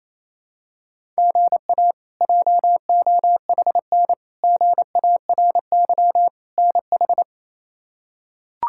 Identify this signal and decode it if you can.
Morse code, heard as a keyed tone, about 28 wpm, 700 Hz. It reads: GA JOHN GARY NH
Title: K1USN SST QSO Practice - 28wpm